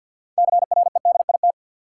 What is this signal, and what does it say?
Morse code, heard as a keyed tone, about 52 wpm, 700 Hz. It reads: CREDIT
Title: Sentences from Top 1000 Words - New Word Drill - 50wpm